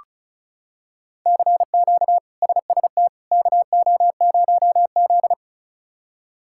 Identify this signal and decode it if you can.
Morse code, heard as a keyed tone, about 34 wpm, 700 Hz. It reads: CQ SST KO0Z
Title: K1USN SST QSO Practice - 35wpm